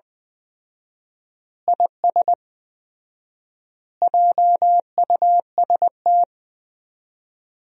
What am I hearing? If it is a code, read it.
Morse code, heard as a keyed tone, about 20 wpm, 700 Hz. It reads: IS JUST
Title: 4x Word-Spacing - Sentences from Top 200 Words - 20wpm